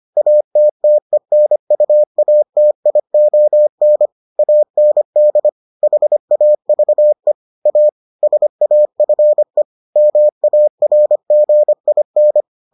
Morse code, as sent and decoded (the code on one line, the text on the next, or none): .- - - . -. ..- .- - .. --- -. / .- -. -.. / .... .- ...- . / .- / ... .- ..-. / -- .- .-. --. .. -.
ATTENUATION AND HAVE A SAFE MARGIN